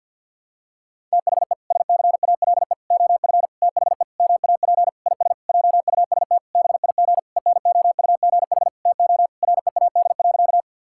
Morse code, heard as a keyed tone, about 51 wpm, 700 Hz. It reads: THE SCALE OF THE MAP IS JUST BIG ENOUGH TO READ.
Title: Sentences from Top 1000 Words - Review Part 5 - 50wpm